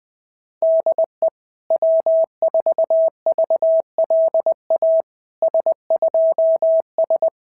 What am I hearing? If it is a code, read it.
Morse code, heard as a keyed tone, about 20 wpm, 650 Hz. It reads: DE W4VLA S2S
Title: SOTA QSO Practice 20wpm